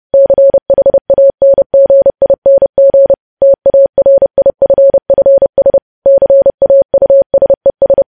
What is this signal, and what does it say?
Morse code, heard as a keyed tone, about 30 wpm, 550 Hz. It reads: CHANGING TARIFFS CAUSES
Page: news headlines in morse code at 30 wpm